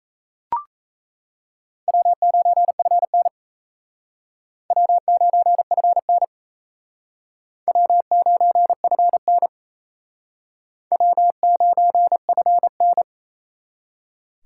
Morse code, heard as a keyed tone, about 35 wpm, 700 Hz. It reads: W9FN W9FN W9FN W9FN